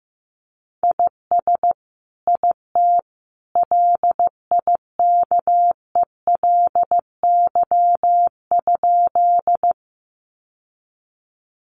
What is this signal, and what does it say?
Morse code, heard as a keyed tone, about 15 wpm, 700 Hz. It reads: IS IT LIKELY?
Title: Sentences from Top 600 Words 15wpm